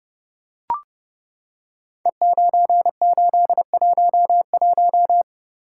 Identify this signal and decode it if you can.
Morse code, heard as a keyed tone, about 30 wpm, 700 Hz. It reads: E9811
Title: Code Groups - 5 Letter:Numbers - 30wpm